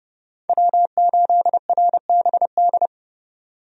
Morse code, heard as a keyed tone, about 30 wpm, 700 Hz. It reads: W8RBD
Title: USA Call Signs - 5 Characters - 30wpm